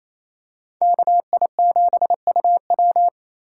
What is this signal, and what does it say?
Morse code, heard as a keyed tone, about 28 wpm, 700 Hz. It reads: KI7UW
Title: USA Call Signs - 5 Characters - 4x Character Spacing - 28wpm